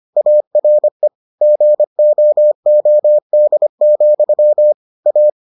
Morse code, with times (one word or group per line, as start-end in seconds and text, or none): 0.2-1.1 ARE
1.4-4.7 GOOD,
5.1-5.3 A